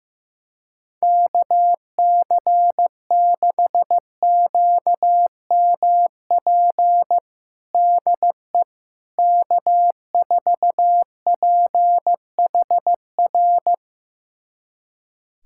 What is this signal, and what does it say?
Morse code, heard as a keyed tone, about 15 wpm, 700 Hz. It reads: KC6QMP DE K4PHR
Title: USA Callsign DE Callsign 15wpm